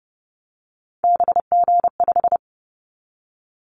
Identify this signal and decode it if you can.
Morse code, heard as a keyed tone, about 30 wpm, 700 Hz. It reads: BG5
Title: Callsign Prefixes for Top 100 DXCC - 30wpm